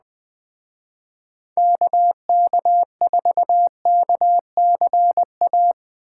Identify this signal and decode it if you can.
Morse code, heard as a keyed tone, about 19 wpm, 700 Hz. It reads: KK4KCA
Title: Speed Racing - USA Call Signs 17wpm